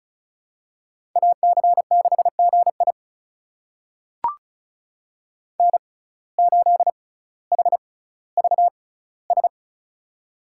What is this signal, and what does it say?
Morse code, heard as a keyed tone, about 36 wpm, spaced as for 18 wpm, 700 Hz. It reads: AC6GI N8HVS
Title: USA Call Signs - 5 Characters - 6x Character Spacing - 35wpm